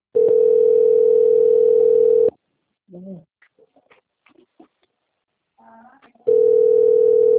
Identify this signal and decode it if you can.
Ringback tone (North America), 440 + 480 Hz.
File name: out- - - - - .73203